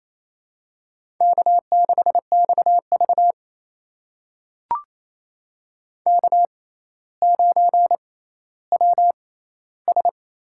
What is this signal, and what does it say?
Morse code, heard as a keyed tone, about 28 wpm, spaced as for 14 wpm, 700 Hz. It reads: K6XV K9WS
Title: USA Call Signs - 4 Characters - 6x Character Spacing - 28wpm